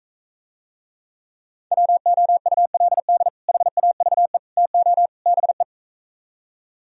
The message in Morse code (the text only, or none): WOULD HAVE TO BE